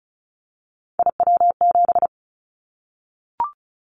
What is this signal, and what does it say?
Morse code, heard as a keyed tone, about 36 wpm, 700 Hz. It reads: IW7
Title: Callsign Prefixes for Top 50 DXCC - 35wpm